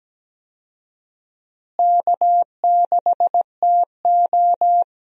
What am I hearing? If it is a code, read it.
Morse code, heard as a keyed tone, about 17 wpm, 700 Hz. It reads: K6TO
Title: USA Call Signs - 4 Characters - Repeated 2x - 17wpm